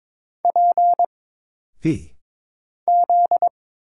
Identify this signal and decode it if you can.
Morse code, heard as a keyed tone, about 22 wpm, 700 Hz. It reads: P Z
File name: Single Character (Letter:Number:Punctuation:Prosign) - Mind-Melt - 22wpm